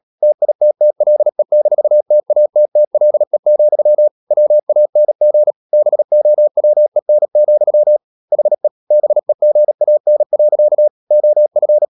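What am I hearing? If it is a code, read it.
Morse code, heard as a keyed tone, about 37 wpm, 600 Hz. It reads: TITTLE-TATTLE, WANG BOWEN, HE BEGAN. OF
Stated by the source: Morse code practice at 35–39 wpm sending English text of Ken Follett's novel Never, from CW med Gnister.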